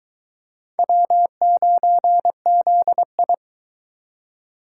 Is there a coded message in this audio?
Yes: Morse code from 0.8 to 3.3 s, heard as a keyed tone, about 22 wpm, 700 Hz. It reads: W9ZI